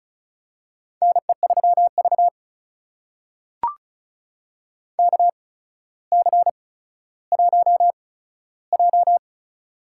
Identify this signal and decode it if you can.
Morse code, heard as a keyed tone, about 36 wpm, spaced as for 15 wpm, 700 Hz. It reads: NE3V KC1J
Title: USA Call Signs - 4 Characters - 8x Character Spacing - 35wpm